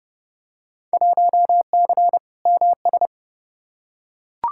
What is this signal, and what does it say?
Morse code, heard as a keyed tone, about 29 wpm, 700 Hz. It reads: 1C MS
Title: Field Day Exchanges Encore - 30wpm